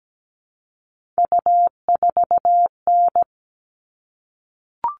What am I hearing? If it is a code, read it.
Morse code, heard as a keyed tone, about 17 wpm, 700 Hz. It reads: U4N
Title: All Cut Numbers - 3-Digit 17wpm